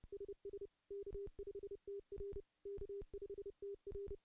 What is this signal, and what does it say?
Morse code, heard as a keyed tone, about 30 wpm, 400 Hz. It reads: SS K5TR K5TR
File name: contests / CW-SS-k5tr- -0400